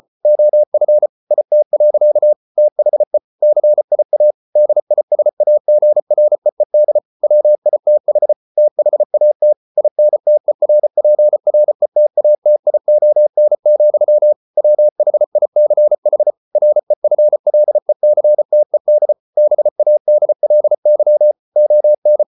Morse code, heard as a keyed tone, about 34 wpm, 600 Hz. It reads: OF IT. THE CIA DISAGREED WITH THAT INTERPRETATION, WHICH REFLECTED BADLY ON